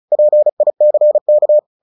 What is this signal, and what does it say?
Morse code, heard as a keyed tone, about 34 wpm, 600 Hz. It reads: PICK
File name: the old man and the sea 2 35wpm